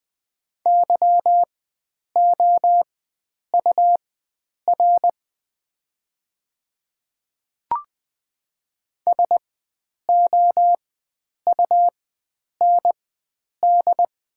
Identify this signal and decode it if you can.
Morse code, heard as a keyed tone, about 20 wpm, spaced as for 9 wpm, 700 Hz. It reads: YOUR SOUND
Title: 4x Character Spacing - Top 100 Words - 20wpm